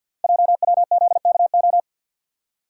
Morse code, heard as a keyed tone, about 50 wpm, 700 Hz. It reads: JWZKO
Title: Code Groups - 5 Letters - 50wpm